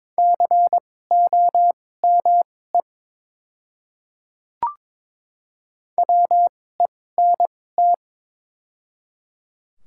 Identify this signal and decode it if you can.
Morse code, heard as a keyed tone, about 22 wpm, spaced as for 16 wpm, 700 Hz. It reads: COME WENT
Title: 2x Character Spacing - Top 500 Words - 22wpm